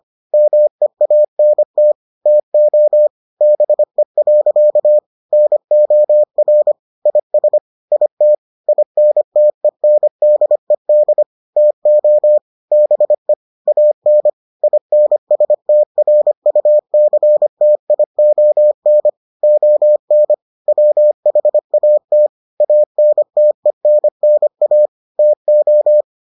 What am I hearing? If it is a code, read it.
Morse code, heard as a keyed tone, about 25 wpm, 600 Hz. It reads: MEANT TO BE. NOR IS IT INTENDED TO BE AN INSTRUCTION ON WHAT ANTENNA TO